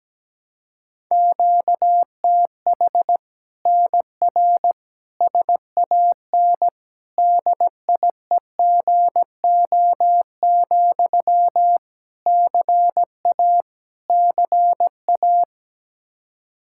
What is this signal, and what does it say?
Morse code, heard as a keyed tone, about 17 wpm, 700 Hz. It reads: QTH NR SAN DIEGO, CA CA
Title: Practice QSO - Exchange 1 - Beginning - 17wpm